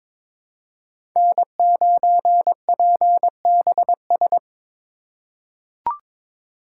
Morse code, heard as a keyed tone, about 21 wpm, 700 Hz. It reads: N9PBS
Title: USA Call Signs - 5 Characters - 4x Character Spacing - 22wpm